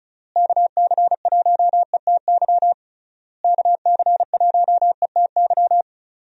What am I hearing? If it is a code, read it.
Morse code, heard as a keyed tone, about 36 wpm, 700 Hz. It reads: KC1ETY KC1ETY
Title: USA Call Signs - 6 Characters - Repeated 3x - 35wpm